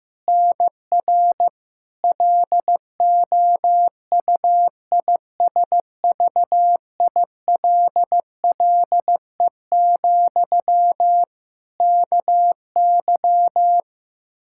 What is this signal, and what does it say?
Morse code, heard as a keyed tone, about 15 wpm, 700 Hz. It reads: NR LOUISVILLE, KY
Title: K1USN SST QSO Practice 15wpm